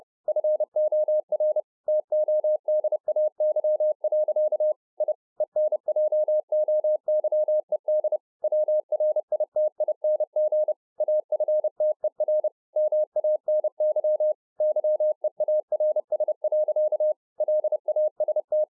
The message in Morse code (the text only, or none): FOR TODAY. I ENJOYED WRITING AFTER MANY YEARS. LAST